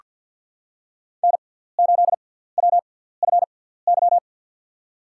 Morse code, heard as a keyed tone, about 48 wpm, spaced as for 17 wpm, 690 Hz. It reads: N8WFY
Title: USA Call Signs - 6x Character Spacing - 50wpm